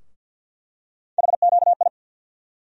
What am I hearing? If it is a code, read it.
Morse code, heard as a keyed tone, about 50 wpm, 690 Hz. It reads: HQI
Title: USA Call Sign Postfixes - 3 Letters 50wpm